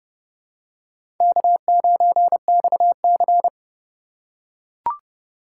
Morse code, heard as a keyed tone, about 30 wpm, 700 Hz. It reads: K9XC
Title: USA Call Signs - 4 Characters - 2x Character Spacing - 30wpm